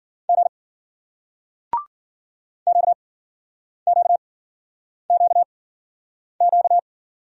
Morse code, heard as a keyed tone, about 53 wpm, spaced as for 9 wpm, 700 Hz. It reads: G QQQQ